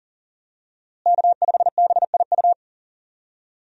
Morse code, heard as a keyed tone, about 40 wpm, 700 Hz. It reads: K5BIU
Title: USA Call Signs - 5 Characters - Repeated 2x - 40wpm